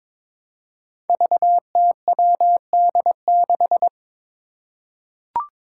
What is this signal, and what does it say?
Morse code, heard as a keyed tone, about 22 wpm, 700 Hz. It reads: VTWD6
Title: Five Challenging Letter-Number Combinations 22wpm